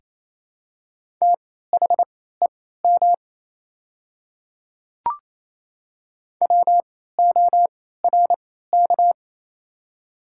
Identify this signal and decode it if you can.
Morse code, heard as a keyed tone, about 28 wpm, spaced as for 16 wpm, 700 Hz. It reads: THEM WORK
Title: 3x Character Spacing - Top 200 Words - 28wpm